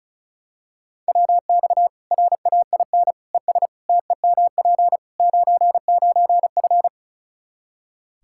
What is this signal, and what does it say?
Morse code, heard as a keyed tone, about 36 wpm, 700 Hz. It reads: WX RAIN ES TEMP 99F
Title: Practice QSO - Exchange 2 - Info - 35wpm